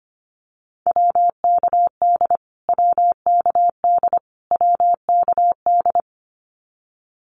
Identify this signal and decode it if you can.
Morse code, heard as a keyed tone, about 25 wpm, 700 Hz. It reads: WKD WKD WKD
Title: Abbreviations Repeated 3x - 25wpm